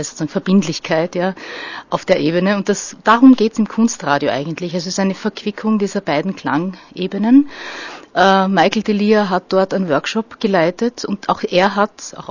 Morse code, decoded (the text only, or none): none